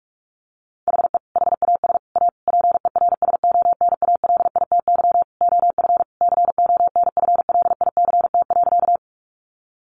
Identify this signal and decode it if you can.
Morse code, heard as a keyed tone, about 44 wpm, 700 Hz. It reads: HE HAS A PERSONALITY OF CONFLICT.
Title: Sentences from Top 1400 Words 45wpm